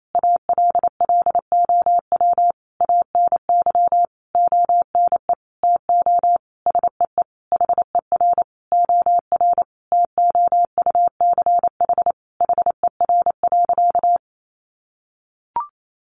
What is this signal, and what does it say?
Morse code, heard as a keyed tone, about 28 wpm, 700 Hz. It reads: ALLOW ANY ONE TO SEE HER OR TOUCH HER.
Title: Sentences from Top 1200 Words 28wpm